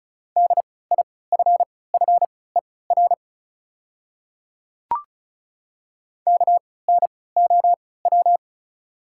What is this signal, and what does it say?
Morse code, heard as a keyed tone, about 36 wpm, spaced as for 20 wpm, 690 Hz. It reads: DIFFER KNOW
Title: 3x Character Spacing - Top 300 Words - 35wpm